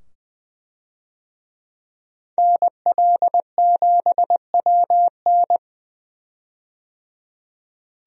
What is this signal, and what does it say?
Morse code, heard as a keyed tone, about 20 wpm, 700 Hz. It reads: NL7WN
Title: USA Call Signs - 5 Characters - 20wpm